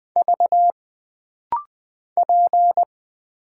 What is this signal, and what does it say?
Morse code, heard as a keyed tone, about 20 wpm, 700 Hz. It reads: V P
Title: Single Letter-Number Alt - Rapid-Fire 20wpm